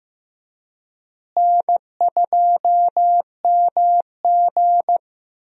Morse code, heard as a keyed tone, about 15 wpm, 700 Hz. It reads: N2MG
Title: USA Call Signs - 4 Characters - Repeated 3x - 15wpm